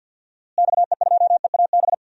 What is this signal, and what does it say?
Morse code, heard as a keyed tone, about 50 wpm, 700 Hz. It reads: XE2EAB
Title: DXCC-100 Call Signs - 6 Characters - Repeated 3x - 50wpm